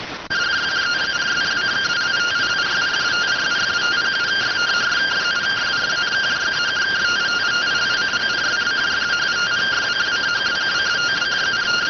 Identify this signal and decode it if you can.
Two-tone teleprinter signal, about 170 Hz shift.